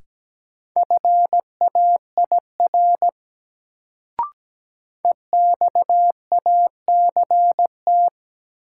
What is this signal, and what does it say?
Morse code, heard as a keyed tone, about 17 wpm, 700 Hz. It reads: FAIR EXACT
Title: Sentences from Top 1300 Words - New Word Drill Alt - 17wpm